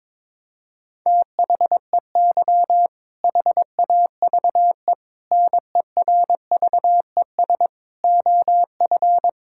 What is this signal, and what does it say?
Morse code, heard as a keyed tone, about 21 wpm, 690 Hz. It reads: THEY HAVE NERVES OF
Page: morse-code-ninja-practice-22wpm